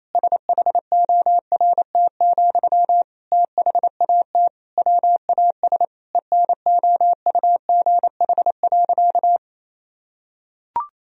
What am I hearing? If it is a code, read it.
Morse code, heard as a keyed tone, about 28 wpm, 700 Hz. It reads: SHORT, THAT WAS ENOUGH.
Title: Sentences from Top 1000 Words - Review Part 2 - 28wpm